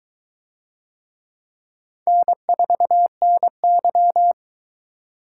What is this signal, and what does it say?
Morse code, heard as a keyed tone, about 22 wpm, 700 Hz. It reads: N4NY